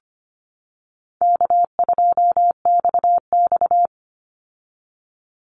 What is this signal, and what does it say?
Morse code, heard as a keyed tone, about 25 wpm, 700 Hz. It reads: K2XX
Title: Easy Call Signs - 25wpm